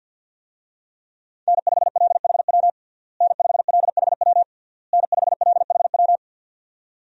Morse code, heard as a keyed tone, about 52 wpm, 700 Hz. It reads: N5LHW N5LHW N5LHW
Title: USA Call Signs - 5 Characters - Repeated 3x - 50wpm